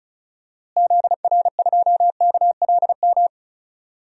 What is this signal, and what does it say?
Morse code, heard as a keyed tone, about 36 wpm, 700 Hz. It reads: ZR2KLM
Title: DXCC-100 Call Signs - 6 Characters - Repeated 2x - 35wpm